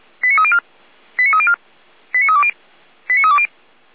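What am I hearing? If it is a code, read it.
Five-tone selective call (ZVEI), heard as quick runs of five tones, tones of about 70 ms. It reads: 89304 89304 89320 89320
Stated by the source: ZVEI five-tone selective call received by SDR.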